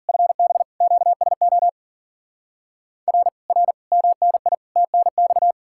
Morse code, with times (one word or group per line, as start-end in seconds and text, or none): 0.1-0.6 FB
0.8-1.7 QSO
3.1-3.3 R
3.5-3.7 R
3.9-4.5 MNI
4.8-5.5 TNX